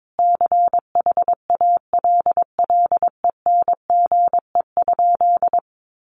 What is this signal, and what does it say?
Morse code, heard as a keyed tone, about 22 wpm, 700 Hz. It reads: CHALLENGE?
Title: Sentences from Top 1300 Words 22wpm